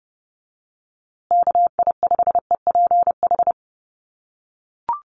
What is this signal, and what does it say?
Morse code, heard as a keyed tone, about 30 wpm, 700 Hz. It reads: KI5EPH
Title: USA Call Signs Repeated 2x 30wpm